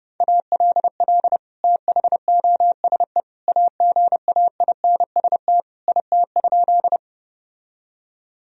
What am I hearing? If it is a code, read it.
Morse code, heard as a keyed tone, about 32 wpm, 700 Hz. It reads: ALL THOSE AGAINST IT?
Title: Sentences from Top 1000 Words - Review Part 1 - 30wpm